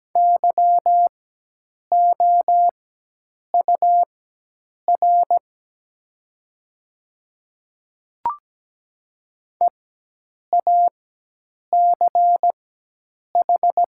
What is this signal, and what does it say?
Morse code, heard as a keyed tone, about 17 wpm, spaced as for 8 wpm, 700 Hz. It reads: YOUR EACH
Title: 4x Character Spacing - Top 100 Words - 17wpm